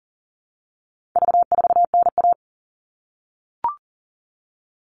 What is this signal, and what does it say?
Morse code, heard as a keyed tone, about 40 wpm, 700 Hz. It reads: V4NA
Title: All Cut Numbers - 4-Digit 40wpm